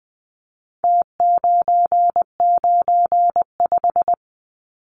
Morse code, heard as a keyed tone, about 20 wpm, 700 Hz. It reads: T995
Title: Numbers - Contest Style - 4-Digit 20wpm